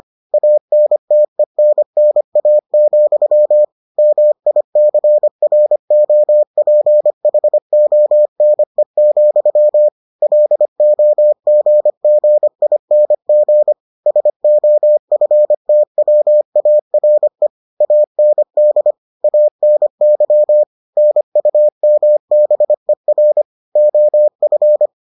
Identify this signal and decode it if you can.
Morse code, heard as a keyed tone, about 25 wpm, 600 Hz. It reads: ANTENNA, MICROPHONE, LOGGING SOFTWARE AND ANY NUMBER OF